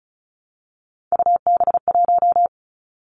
Morse code, heard as a keyed tone, about 36 wpm, 700 Hz. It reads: UB1
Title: Three Challenging Letter-Number Combinations 35wpm